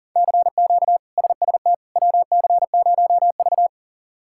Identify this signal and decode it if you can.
Morse code, heard as a keyed tone, about 40 wpm, 690 Hz. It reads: CQ SST WC0V
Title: K1USN SST QSO Practice - 40wpm